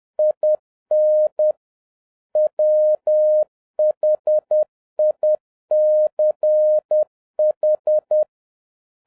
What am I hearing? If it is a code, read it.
Morse code, heard as a keyed tone, about 10 wpm, 610 Hz. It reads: IN WHICH
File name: 2011-07-mp3 / n41 EXCITING QRP CW OPERATION 10wpm